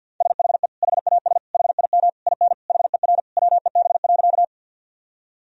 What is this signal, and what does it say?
Morse code, heard as a keyed tone, about 50 wpm, 700 Hz. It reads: SHE HAS HIM IN HER WEB.
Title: Sentences from Top 1000 Words - Review Part 4 - 50wpm